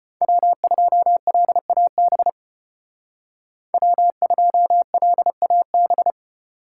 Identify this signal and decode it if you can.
Morse code, heard as a keyed tone, about 32 wpm, 700 Hz. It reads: W2LAB W2LAB